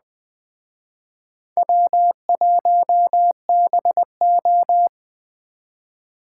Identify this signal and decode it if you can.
Morse code, heard as a keyed tone, about 20 wpm, 700 Hz. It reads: W1BO
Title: Speed Racing - USA Call Signs 17wpm